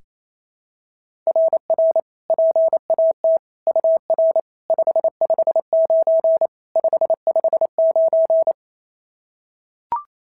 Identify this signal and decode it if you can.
Morse code, heard as a keyed tone, about 29 wpm, 690 Hz. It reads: RR PAT UR 559 559
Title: SOTA QSO Practice 28wpm